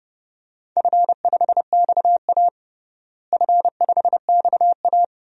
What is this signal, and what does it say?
Morse code, heard as a keyed tone, about 32 wpm, 700 Hz. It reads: F5XA F5XA
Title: DXCC-100 Call Signs - 4 Characters - Repeated 3x - 30wpm